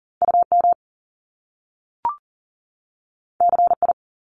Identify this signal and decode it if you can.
Morse code, heard as a keyed tone, about 41 wpm, 700 Hz. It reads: UM CI
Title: Top Two Letter Combinations in Words 40wpm